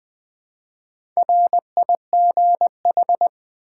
Morse code, heard as a keyed tone, about 20 wpm, 700 Hz. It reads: RIGH
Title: Build A Word - 1 Letter At A Time - 3 to 5-Letter Words - 20wpm